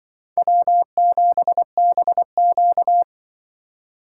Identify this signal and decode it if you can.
Morse code, heard as a keyed tone, about 25 wpm, 700 Hz. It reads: W7BQ
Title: Speed Racing - USA 1x2 Call Signs 20wpm